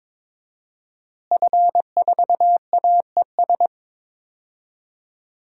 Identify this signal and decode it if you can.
Morse code, heard as a keyed tone, about 21 wpm, 700 Hz. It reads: F4AES
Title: DXCC-100 Call Signs - 5 Characters - Repeated 2x - 22wpm